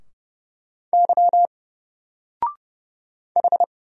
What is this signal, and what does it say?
Morse code, heard as a keyed tone, about 31 wpm, 700 Hz. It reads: Y H
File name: Single-Letter-Number-30wpm